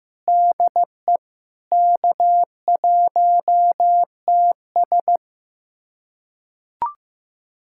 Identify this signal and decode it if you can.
Morse code, heard as a keyed tone, about 15 wpm, 700 Hz. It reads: DE K1TS
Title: USA Callsign DE Callsign - 4 Characters - 15wpm